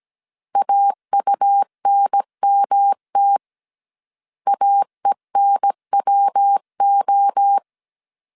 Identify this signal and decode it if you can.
Morse code, heard as a keyed tone, about 17 wpm, 800 Hz. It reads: AUNMT AENWO